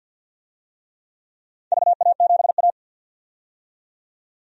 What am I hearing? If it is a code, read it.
Morse code, heard as a keyed tone, about 47 wpm, 700 Hz. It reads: VA7A
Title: DXCC-100 Call Signs - 4 Characters - 50wpm